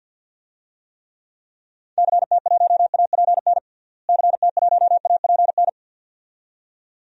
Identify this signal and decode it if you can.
Morse code, heard as a keyed tone, about 51 wpm, 700 Hz. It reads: CT1APN CT1APN
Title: DXCC-100 Call Signs - 6 Characters - Repeated 2x - 50wpm